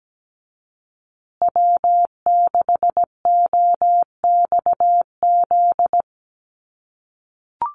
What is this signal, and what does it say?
Morse code, heard as a keyed tone, about 17 wpm, 700 Hz. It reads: W6OXZ
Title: USA Call Signs Repeated 2x 17wpm